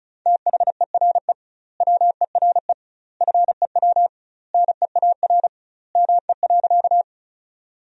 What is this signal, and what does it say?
Morse code, heard as a keyed tone, about 36 wpm, 700 Hz. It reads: THERE WERE FEW NEAR ME.
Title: 2x Word-Spacing - Sentences from Top 300 Words - 35wpm